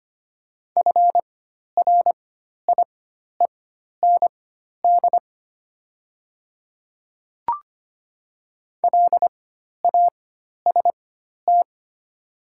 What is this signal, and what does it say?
Morse code, heard as a keyed tone, about 25 wpm, spaced as for 12 wpm, 700 Hz. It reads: FRIEND LAST
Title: 4x Character Spacing - Top 300 Words - 25wpm